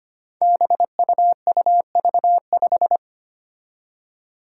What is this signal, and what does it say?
Morse code, heard as a keyed tone, about 25 wpm, 700 Hz. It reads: BUUV5
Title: Five Challenging Letter-Number Combinations 25wpm